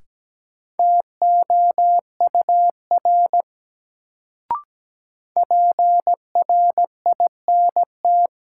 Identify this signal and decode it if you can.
Morse code, heard as a keyed tone, about 17 wpm, 700 Hz. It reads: TOUR PRINT
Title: Sentences from Top 1400 Words - New Word Drill - 17wpm